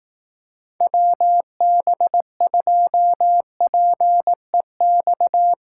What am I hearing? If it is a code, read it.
Morse code, heard as a keyed tone, about 18 wpm, 700 Hz. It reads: WB2PEX
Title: Speed Racing - USA 2x3 Call Signs 15wpm